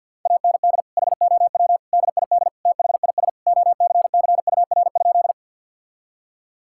Morse code, heard as a keyed tone, about 50 wpm, 700 Hz. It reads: AND HOW DID THIS OCCUR?
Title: Sentences from Top 1000 Words - Review Part 6 - 50wpm